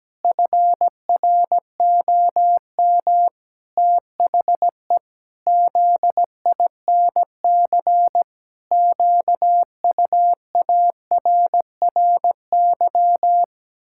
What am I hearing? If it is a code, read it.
Morse code, heard as a keyed tone, about 17 wpm, 700 Hz. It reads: FROM THE ZINC QUARRY
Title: Pangram Challenge v2 - 17wpm